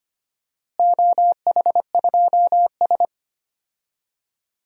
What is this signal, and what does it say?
Morse code, heard as a keyed tone, about 25 wpm, 700 Hz. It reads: OH2S
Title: DXCC-100 Call Signs - 4 Characters - Repeated 2x - 25wpm